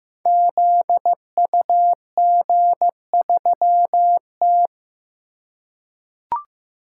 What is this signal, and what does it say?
Morse code, heard as a keyed tone, about 15 wpm, 700 Hz. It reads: ZUG3T
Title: Code Groups - 5 Letter:Numbers - 15wpm